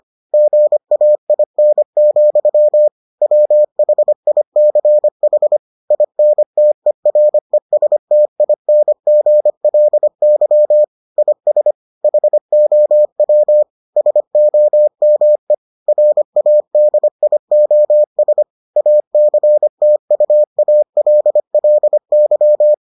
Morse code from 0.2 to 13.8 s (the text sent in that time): GAIN, WHICH INTERESTINGLY IS HOW